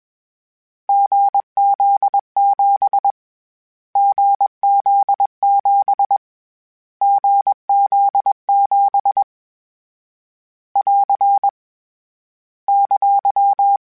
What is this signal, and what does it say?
Morse code, heard as a keyed tone, about 21 wpm, 800 Hz. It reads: GZ7 GZ7 GZ7 <AR> !
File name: Short Runs / Short Runs-mp3s / GZ GZ7 @20